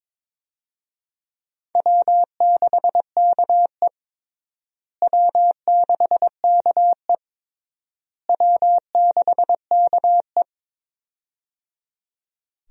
Morse code, heard as a keyed tone, about 21 wpm, 700 Hz. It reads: W6KE W6KE W6KE
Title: USA Call Signs - 4 Characters - Repeated 3x - 22wpm